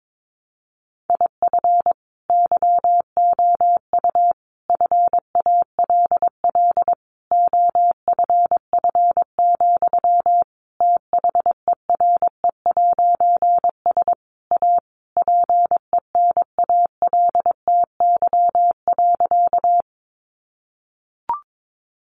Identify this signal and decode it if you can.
Morse code, heard as a keyed tone, about 22 wpm, 700 Hz. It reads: IF YOU FALL OFF, THERE'S A PENALTY.
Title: A Sound Of Thunder 22wpm